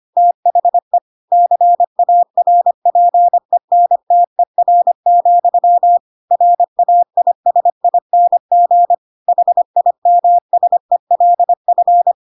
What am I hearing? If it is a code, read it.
Morse code, heard as a keyed tone, about 25 wpm, 700 Hz. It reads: THE CARPENTER, RAISING HIMSELF
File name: Pinocchio / Pinocchio ch2 25 25 ik1hge